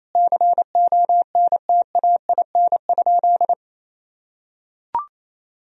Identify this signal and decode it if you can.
Morse code, heard as a keyed tone, about 28 wpm, 700 Hz. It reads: CONTAIN?
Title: Sentences from Top 1000 Words - Review Part 2 - 28wpm